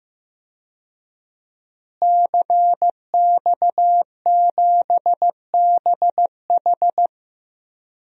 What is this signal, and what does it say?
Morse code, heard as a keyed tone, about 15 wpm, 700 Hz. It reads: CX7BH
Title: DXCC-100 Call Signs - 5 Characters - Repeated 2x - 15wpm